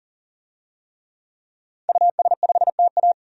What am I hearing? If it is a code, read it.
Morse code, heard as a keyed tone, about 40 wpm, 700 Hz. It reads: US5TA
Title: DXCC-100 Call Signs - 5 Characters - 40wpm